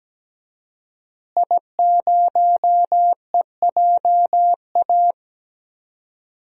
Morse code, heard as a keyed tone, about 17 wpm, 700 Hz. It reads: I0EJA
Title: DXCC-100 Call Signs - 5 Characters - Repeated 3x - 17wpm